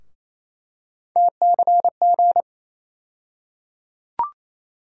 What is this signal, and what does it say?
Morse code, heard as a keyed tone, about 28 wpm, 700 Hz. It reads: TCG
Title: USA Call Sign Postfixes - 3 Letters 28wpm